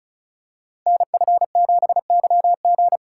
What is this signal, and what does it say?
Morse code, heard as a keyed tone, about 36 wpm, 700 Hz. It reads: NF7YG
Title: Code Groups - 5 Letter:Numbers - 35wpm